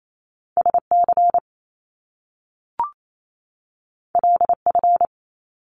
Morse code, heard as a keyed tone, about 28 wpm, 700 Hz. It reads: SC LF
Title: Top Two Letter Combinations in Words 28wpm